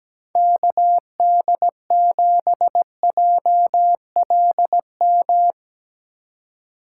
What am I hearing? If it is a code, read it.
Morse code, heard as a keyed tone, about 17 wpm, 700 Hz. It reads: KD7JLM
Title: Speed Racing - USA Call Signs 17wpm